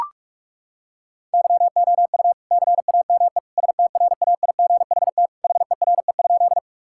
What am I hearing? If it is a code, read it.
Morse code, heard as a keyed tone, about 41 wpm, 700 Hz. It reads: YOU CAME STRAIGHT HERE?
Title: Sentences from Top 1000 Words - Review Part 6 - 45wpm